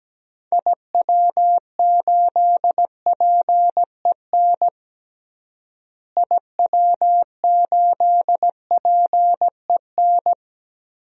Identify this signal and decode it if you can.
Morse code, heard as a keyed tone, about 17 wpm, 700 Hz. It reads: IW8PEN IW8PEN
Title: DXCC-100 Call Signs - 6 Characters - Repeated 2x - 17wpm